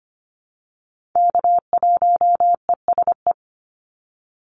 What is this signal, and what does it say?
Morse code, heard as a keyed tone, about 25 wpm, 700 Hz. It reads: K1ESE
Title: Call Signs - 25wpm